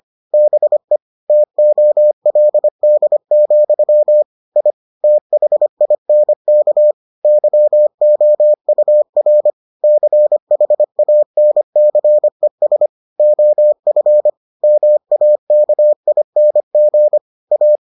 Morse code, heard as a keyed tone, about 25 wpm, 600 Hz. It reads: BE TOLD, I THINK YOUR CHANCES OF MAKING A